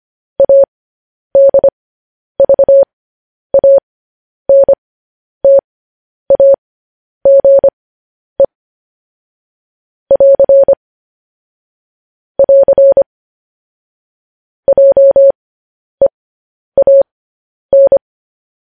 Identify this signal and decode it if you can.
Morse code, heard as a keyed tone, about 25 wpm, spaced as for 10 wpm, 550 Hz. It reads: ADVANTAGE <AR> <AR> JEAN
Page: news headlines in morse code at 10 wpm